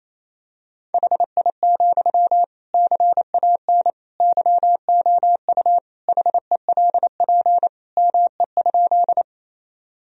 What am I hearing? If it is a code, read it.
Morse code, heard as a keyed tone, about 28 wpm, 700 Hz. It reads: HI, CAN YOU HELP ME?
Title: Sentences from Top 1000 Words - Review Part 3 - 28wpm